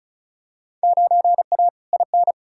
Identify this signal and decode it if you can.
Morse code, heard as a keyed tone, about 36 wpm, 700 Hz. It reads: 9A IN
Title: Field Day Exchanges Encore - 35wpm